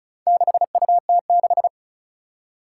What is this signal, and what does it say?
Morse code, heard as a keyed tone, about 36 wpm, 700 Hz. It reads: 6UT6
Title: Four Challenging Letter-Number Combinations 35wpm